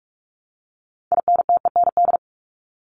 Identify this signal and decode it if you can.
Morse code, heard as a keyed tone, about 44 wpm, 700 Hz. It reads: INTEND
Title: Top 1501-1750 Words 45wpm